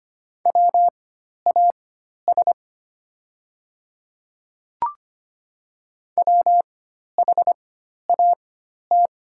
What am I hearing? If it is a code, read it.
Morse code, heard as a keyed tone, about 25 wpm, spaced as for 12 wpm, 700 Hz. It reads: WAS WHAT
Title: 4x Character Spacing - Top 100 Words - 25wpm